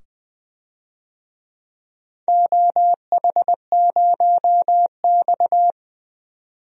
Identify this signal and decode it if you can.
Morse code, heard as a keyed tone, about 20 wpm, 700 Hz. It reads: OH0X
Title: DXCC-100 Call Signs - 4 Characters - 20wpm